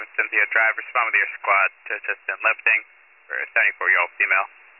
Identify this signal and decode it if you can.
Emergency Alert System (SAME) tones, heard as harsh buzzing bursts, with speech. It is heard from 0.5 to 3.7 s.